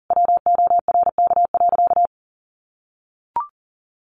Morse code, heard as a keyed tone, about 40 wpm, 700 Hz. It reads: WORK.
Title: Sentences from Top 1100 Words 40wpm